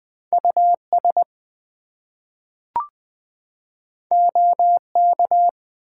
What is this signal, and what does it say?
Morse code, heard as a keyed tone, about 20 wpm, 700 Hz. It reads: US OK
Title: Two Letter Words - 20wpm